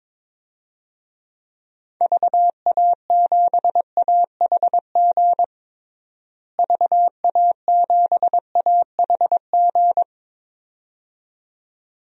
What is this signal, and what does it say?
Morse code, heard as a keyed tone, about 21 wpm, 700 Hz. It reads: VA7AHG VA7AHG
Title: DXCC-100 Call Signs - 6 Characters - Repeated 2x - 22wpm